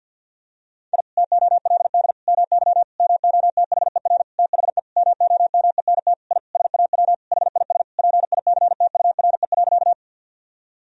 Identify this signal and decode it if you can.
Morse code, heard as a keyed tone, about 52 wpm, 700 Hz. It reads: I TOLD MY MOTHER THE MOMENT I SAW HIS PICTURE.
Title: Sentences from Top 1000 Words - Review Part 1 - 50wpm